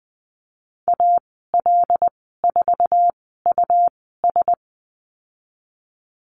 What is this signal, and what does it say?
Morse code, heard as a keyed tone, about 20 wpm, spaced as for 15 wpm, 700 Hz. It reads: AL4US
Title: 140 Lesson 23g - Letter f - Call Signs 20wpm